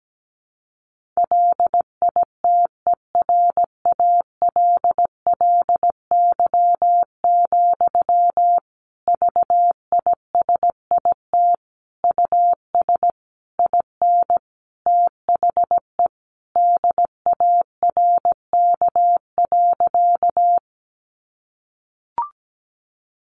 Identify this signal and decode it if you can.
Morse code, heard as a keyed tone, about 17 wpm, 700 Hz. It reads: LITERALLY, VISIT US IN THE DARK.
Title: Sentences from Top 1400 Words 17wpm